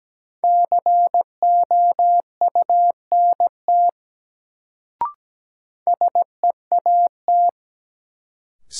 Morse code, heard as a keyed tone, about 17 wpm, 700 Hz. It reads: COUNT SEAT
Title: Sentences from Top 1500 Words - New Word Drill - 17wpm